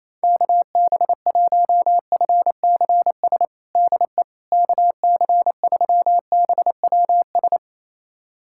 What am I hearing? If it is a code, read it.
Morse code, heard as a keyed tone, about 28 wpm, 700 Hz. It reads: KB1FCS DE KC3BWS
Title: USA Callsign DE Callsign 28wpm